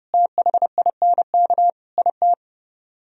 Morse code, heard as a keyed tone, about 31 wpm, 700 Hz. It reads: THINK IT